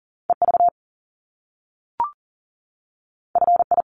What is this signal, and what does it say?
Morse code, heard as a keyed tone, about 39 wpm, 700 Hz. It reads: EV FI
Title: Top Two Letter Combinations in Words 40wpm